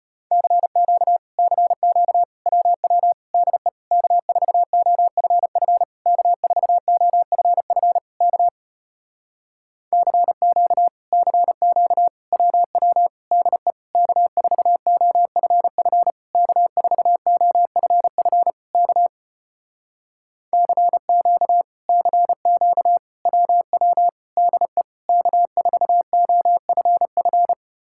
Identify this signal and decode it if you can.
Morse code, heard as a keyed tone, about 33 wpm, 700 Hz. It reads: CQ CQ WW DE K4OFF K4OFF K CQ CQ WW DE K4OFF K4OFF K CQ CQ WW DE K4OFF